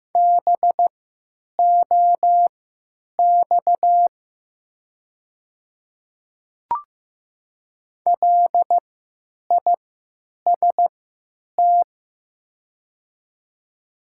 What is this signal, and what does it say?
Morse code, heard as a keyed tone, about 15 wpm, spaced as for 9 wpm, 700 Hz. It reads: BOX LIST
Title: 3x Character Spacing - Top 500 Words - 15wpm